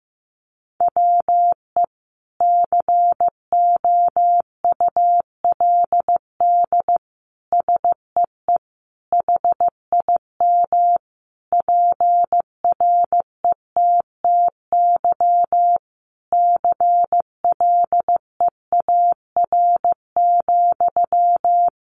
Morse code, heard as a keyed tone, about 15 wpm, 700 Hz. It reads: WE COULD SEE HIM PRETTY CLEAR,
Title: Sentences from Top 1000 Words 15wpm